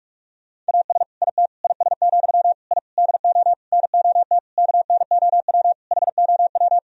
Morse code, heard as a keyed tone, about 48 wpm, 700 Hz. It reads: AS IT IS, I DO NOT KNOW HOW
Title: Sentences from Top 1000 Words - Review Part 2 - 45wpm